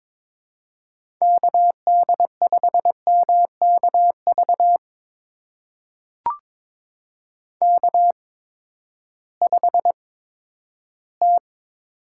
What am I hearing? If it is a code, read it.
Morse code, heard as a keyed tone, about 22 wpm, spaced as for 12 wpm, 700 Hz. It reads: KD5MKV K5T
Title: USA Call Signs - 8x Character Spacing - 22wpm